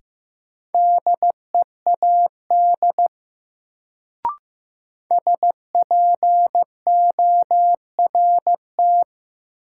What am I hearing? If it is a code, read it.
Morse code, heard as a keyed tone, about 16 wpm, 700 Hz. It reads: DEAD SPORT
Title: Sentences from Top 1500 Words - New Word Drill Alt - 15wpm